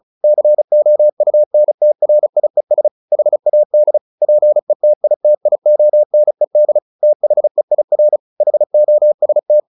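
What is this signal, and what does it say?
Morse code, heard as a keyed tone, about 35 wpm, 600 Hz. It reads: COUNTRIES HAD PETITIONED THEIR HOST
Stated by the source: Morse practice at 35 wpm from CW med Gnister, sending Ken Follett's novel Never in English.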